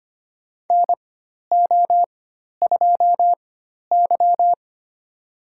Morse code, heard as a keyed tone, about 25 wpm, spaced as for 12 wpm, 700 Hz. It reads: NO2Y
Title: USA Call Signs - 4x Character Spacing - 25wpm